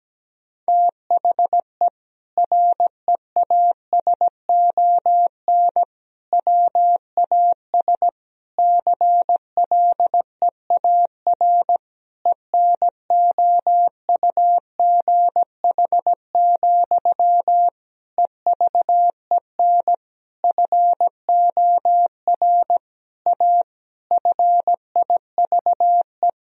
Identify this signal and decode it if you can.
Morse code, heard as a keyed tone, about 17 wpm, 700 Hz. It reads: THE REASON WAS CLEAR ENOUGH, EVEN FOR A FIVE
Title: Sentences from Top 1000 Words - Review Part 2 - 17wpm